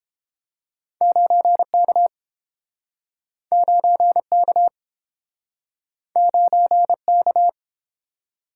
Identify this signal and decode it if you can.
Morse code, heard as a keyed tone, about 31 wpm, 700 Hz. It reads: 9K 9K 9K